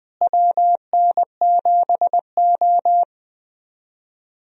Morse code, heard as a keyed tone, about 20 wpm, 700 Hz. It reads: WN7O
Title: USA Call Signs - 4 Characters - Repeated 3x - 20wpm